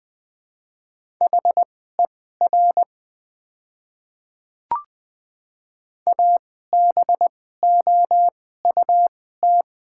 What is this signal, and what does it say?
Morse code, heard as a keyed tone, about 20 wpm, spaced as for 14 wpm, 700 Hz. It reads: HER ABOUT
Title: 2x Character Spacing - Top 200 Words - 20wpm